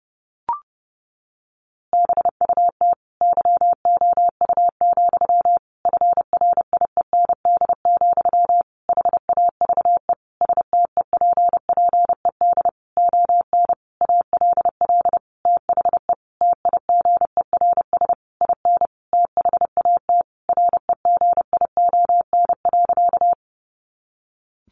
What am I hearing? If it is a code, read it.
Morse code, heard as a keyed tone, about 30 wpm, 700 Hz. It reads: BUT YOU, FRIEND, HAVE STEPPED ON ALL THE TIGERS IN THAT REGION.
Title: A Sound Of Thunder 30wpm